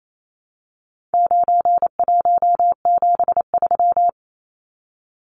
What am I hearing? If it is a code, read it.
Morse code, heard as a keyed tone, about 28 wpm, 700 Hz. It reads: 9173
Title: Numbers - 4-Digit 28wpm